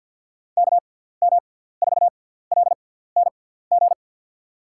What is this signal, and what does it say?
Morse code, heard as a keyed tone, about 53 wpm, spaced as for 17 wpm, 690 Hz. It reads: KM4LNG
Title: USA Call Signs - 6x Character Spacing - 50wpm